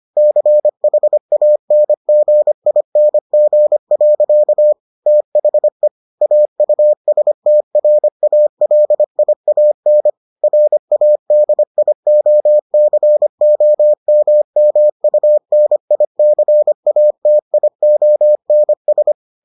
Morse code, heard as a keyed tone, about 25 wpm, 600 Hz. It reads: CHANGING. THE AUSTRALIAN RADIOCOMMUNICATIONS